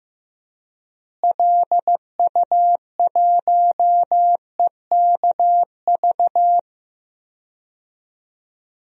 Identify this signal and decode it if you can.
Morse code, heard as a keyed tone, about 15 wpm, 700 Hz. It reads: LU1EKV
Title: DXCC-100 Call Signs - 6 Characters - Repeated 3x - 15wpm